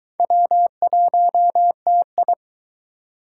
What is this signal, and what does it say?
Morse code, heard as a keyed tone, about 22 wpm, 700 Hz. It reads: W1TI